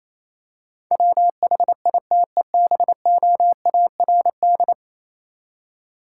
Morse code, heard as a keyed tone, about 27 wpm, 700 Hz. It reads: WHITEBOARD
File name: Easy Words - Speed-Racing with Hints - 22wpm